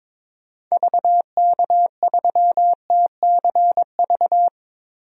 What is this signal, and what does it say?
Morse code, heard as a keyed tone, about 22 wpm, 700 Hz. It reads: VK3TCV
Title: DXCC-100 Call Signs - 6 Characters - Repeated 2x - 22wpm